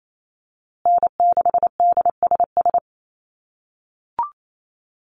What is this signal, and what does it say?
Morse code, heard as a keyed tone, about 28 wpm, 700 Hz. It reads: N6DSS
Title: USA Call Signs Repeated 2x 28wpm